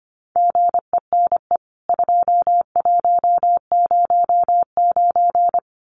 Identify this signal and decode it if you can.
Morse code, heard as a keyed tone, about 25 wpm, 700 Hz. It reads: GENE 2109
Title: CWT Exchange 25wpm